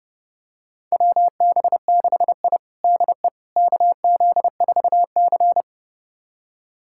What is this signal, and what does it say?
Morse code, heard as a keyed tone, about 30 wpm, 700 Hz. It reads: WB6I DE KZ4C
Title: USA Callsign DE Callsign - 4 Characters - 30wpm